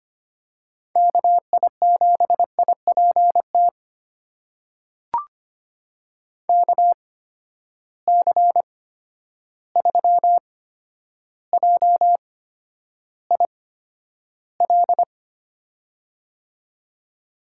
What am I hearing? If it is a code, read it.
Morse code, heard as a keyed tone, about 25 wpm, spaced as for 11 wpm, 700 Hz. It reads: KI7IPT KC3JIL
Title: USA Call Signs - 6 Characters - 8x Character Spacing - 25wpm